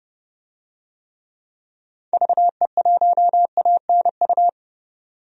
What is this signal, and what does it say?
Morse code, heard as a keyed tone, about 30 wpm, 700 Hz. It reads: VE1ANU
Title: DXCC-100 Call Signs - 6 Characters - 30wpm